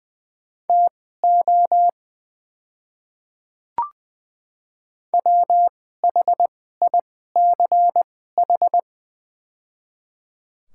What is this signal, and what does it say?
Morse code, heard as a keyed tone, about 20 wpm, spaced as for 14 wpm, 700 Hz. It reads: TO WHICH
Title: 2x Character Spacing - Top 100 Words - 20wpm